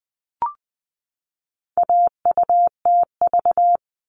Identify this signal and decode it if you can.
Morse code, heard as a keyed tone, about 20 wpm, 700 Hz. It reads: AUTV
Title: All Cut Numbers - 4-Digit 20wpm